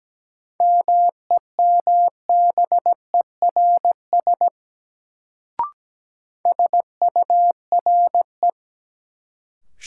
Morse code, heard as a keyed tone, about 17 wpm, 700 Hz. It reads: MEMBERS SURE
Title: Sets of 3 Words - New Word Drill - 17wpm